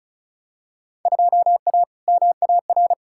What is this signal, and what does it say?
Morse code, heard as a keyed tone, about 34 wpm, 700 Hz. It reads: 2A MAR
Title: Field Day Exchanges - 35wpm